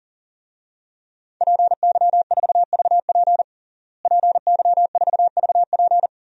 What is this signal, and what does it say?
Morse code, heard as a keyed tone, about 40 wpm, 700 Hz. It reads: PY4VP PY4VP
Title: DXCC-100 Call Signs - 5 Characters - Repeated 3x - 40wpm